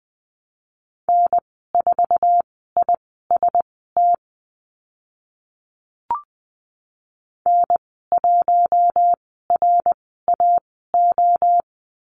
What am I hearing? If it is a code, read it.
Morse code, heard as a keyed tone, about 20 wpm, spaced as for 14 wpm, 700 Hz. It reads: N4IST N1RAO
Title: 062 Lesson 11g - Letter h - Call Signs 20wpm